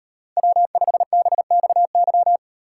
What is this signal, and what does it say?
Morse code, heard as a keyed tone, about 38 wpm, 700 Hz. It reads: W5BXY